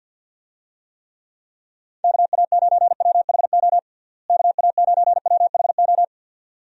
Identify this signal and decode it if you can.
Morse code, heard as a keyed tone, about 51 wpm, 700 Hz. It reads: KA9WHO KA9WHO
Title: USA Call Signs - 6 Characters - Repeated 2x - 50wpm